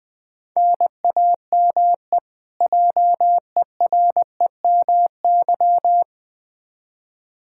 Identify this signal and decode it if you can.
Morse code, heard as a keyed tone, about 20 wpm, 700 Hz. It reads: NAME JEREMY
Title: K1USN SST QSO Practice 20wpm